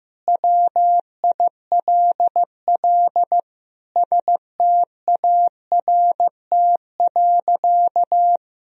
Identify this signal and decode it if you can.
Morse code, heard as a keyed tone, about 15 wpm, 700 Hz. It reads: WILL START.
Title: Sentences from Top 300 Words and Abbreviations 15wpm